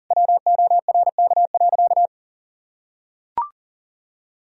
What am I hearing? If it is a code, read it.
Morse code, heard as a keyed tone, about 37 wpm, 700 Hz. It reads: WORK.
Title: Sentences from Top 1000 Words - Review Part 4 - 40wpm